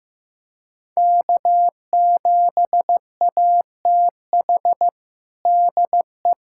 Morse code, heard as a keyed tone, about 15 wpm, 700 Hz. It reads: K7ATH DE
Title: USA Callsign DE Callsign - 5 Characters - 15wpm